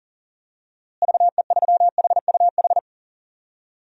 Morse code, heard as a keyed tone, about 37 wpm, 700 Hz. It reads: VE3HUH
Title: DXCC-100 Call Signs - 6 Characters - Repeated 3x - 40wpm